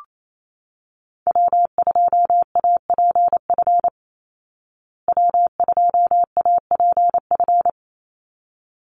Morse code, heard as a keyed tone, about 28 wpm, 700 Hz. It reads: W2APF W2APF
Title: Call Signs Repeated 3x - 28wpm